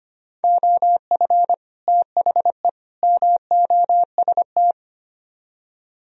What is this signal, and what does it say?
Morse code, heard as a keyed tone, about 32 wpm, 700 Hz. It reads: OF THE MOST